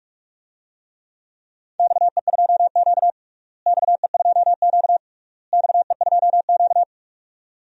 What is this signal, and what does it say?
Morse code, heard as a keyed tone, about 46 wpm, 700 Hz. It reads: XE2Q XE2Q XE2Q
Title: DXCC-100 Call Signs - 4 Characters - Repeated 3x - 45wpm